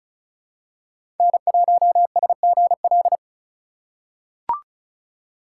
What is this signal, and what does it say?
Morse code, heard as a keyed tone, about 34 wpm, 700 Hz. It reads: N1SGL
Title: USA Call Signs - 5 Characters - 4x Character Spacing - 35wpm